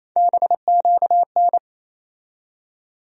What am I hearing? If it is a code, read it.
Morse code, heard as a keyed tone, about 28 wpm, 700 Hz. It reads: BQN
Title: USA Call Sign Postfixes - 3 Letters 28wpm